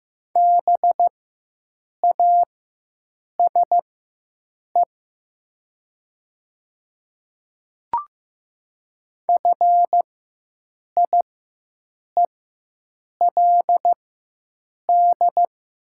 Morse code, heard as a keyed tone, about 15 wpm, spaced as for 7 wpm, 700 Hz. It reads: BASE FIELD
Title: 4x Character Spacing - Top 500 Words - 15wpm